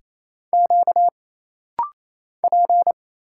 Morse code, heard as a keyed tone, about 28 wpm, 700 Hz. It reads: Q P
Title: Single Letters - Rapid-Fire 28wpm